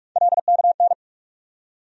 Morse code, heard as a keyed tone, about 46 wpm, 700 Hz. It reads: LKN
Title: USA Call Sign Postfixes - 3 Letters 45wpm